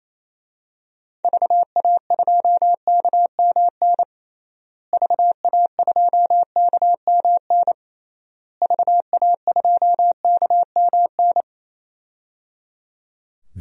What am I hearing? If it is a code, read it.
Morse code, heard as a keyed tone, about 27 wpm, 700 Hz. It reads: VA2KMN VA2KMN VA2KMN
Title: DXCC-100 Call Signs - 6 Characters - Repeated 3x - 28wpm